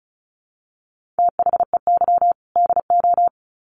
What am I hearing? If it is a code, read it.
Morse code, heard as a keyed tone, about 34 wpm, 700 Hz. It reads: THEY DO
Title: Sets of 2 Words 35wpm